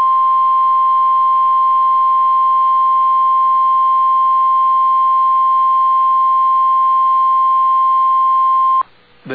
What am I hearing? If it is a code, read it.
Emergency Alert System (SAME) tones (the 1050 Hz attention tone). It is heard from 0.9 to 8.7 s.